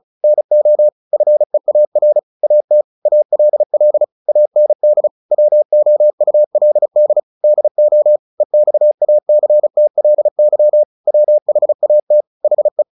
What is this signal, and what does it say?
Morse code, heard as a keyed tone, about 36 wpm, 600 Hz. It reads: NO FEAR AT ALL AND WOULD DO EXACTLY WHAT HE
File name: the old man and the sea 2 35wpm